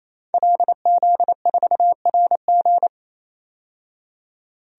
Morse code, heard as a keyed tone, about 28 wpm, 700 Hz. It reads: LZ4RG
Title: DXCC-100 Call Signs - 5 Characters - 28wpm